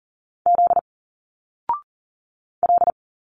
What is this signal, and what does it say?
Morse code, heard as a keyed tone, about 39 wpm, 700 Hz. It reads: Z L
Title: Single Letters 40wpm